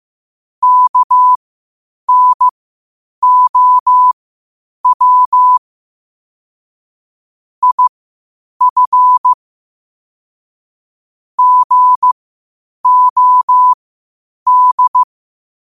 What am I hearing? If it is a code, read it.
Morse code, heard as a keyed tone, about 15 wpm, spaced as for 8 wpm, 1000 Hz. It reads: KNOW IF GOD
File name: qotdfiles / 8wpm